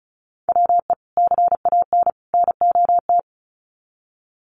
Morse code, heard as a keyed tone, about 34 wpm, 700 Hz. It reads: WE CAN NOT
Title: Sets of 3 Words 35wpm